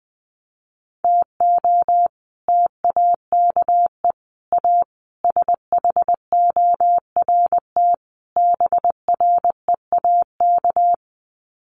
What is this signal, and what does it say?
Morse code, heard as a keyed tone, about 19 wpm, 700 Hz. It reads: TO TAKE A SHORT BREAK
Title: Sets of 5 Words Encore2 20wpm